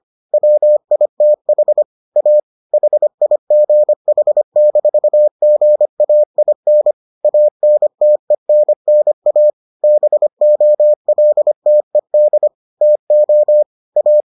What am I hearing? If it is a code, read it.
Morse code, heard as a keyed tone, about 25 wpm, 600 Hz. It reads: WITH A HIGH-GAIN ANTENNA BOLTED TO A